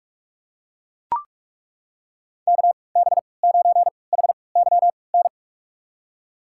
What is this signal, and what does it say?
Morse code, heard as a keyed tone, about 43 wpm, spaced as for 25 wpm, 700 Hz. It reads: KB9HYN
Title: USA Call Signs - 6 Characters - 3x Character Spacing - 45wpm